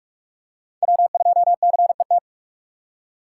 Morse code, heard as a keyed tone, about 42 wpm, 700 Hz. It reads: W2CET
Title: USA Call Signs - 5 Characters - 6x Character Spacing - 45wpm